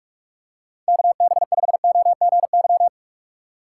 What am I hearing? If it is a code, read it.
Morse code, heard as a keyed tone, about 41 wpm, 700 Hz. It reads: KB5OGY